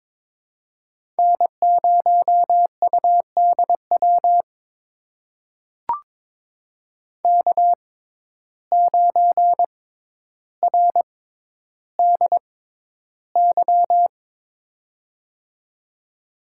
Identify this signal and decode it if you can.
Morse code, heard as a keyed tone, about 23 wpm, spaced as for 11 wpm, 700 Hz. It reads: N0UDW K9RDY
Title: USA Call Signs - 5 Characters - 6x Character Spacing - 22wpm